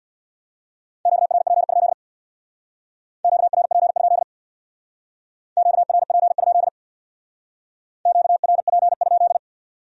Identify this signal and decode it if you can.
Morse code, heard as a keyed tone, about 61 wpm, 700 Hz. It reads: QRP? QRP? QRP? QRP?
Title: Speed-Racing Q-Codes - 50wpm